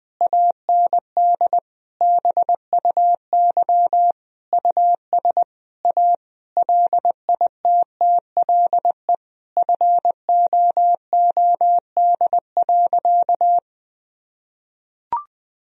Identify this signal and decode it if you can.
Morse code, heard as a keyed tone, about 20 wpm, 700 Hz. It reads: AND BUY US A LITTLE FOOD.
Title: Sentences from Top 1000 Words - Review Part 4 - 20wpm